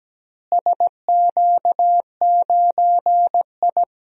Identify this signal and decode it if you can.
Morse code, heard as a keyed tone, about 17 wpm, 700 Hz. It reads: SQ9I
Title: DXCC-100 Call Signs - 4 Characters - Repeated 2x - 17wpm